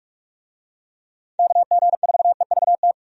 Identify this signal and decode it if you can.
Morse code, heard as a keyed tone, about 43 wpm, 700 Hz. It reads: KG4EVT
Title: USA Call Signs - 6 Characters - 45wpm